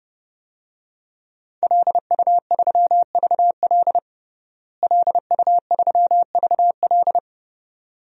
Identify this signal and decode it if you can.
Morse code, heard as a keyed tone, about 30 wpm, 700 Hz. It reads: LU3VL LU3VL
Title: DXCC-100 Call Signs - 5 Characters - Repeated 2x - 30wpm